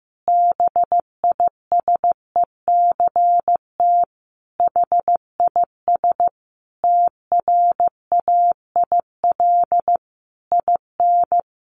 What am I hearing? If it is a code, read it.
Morse code, heard as a keyed tone, about 15 wpm, 700 Hz. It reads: BISECT HIS TRAIL IN
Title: A Sound Of Thunder 15wpm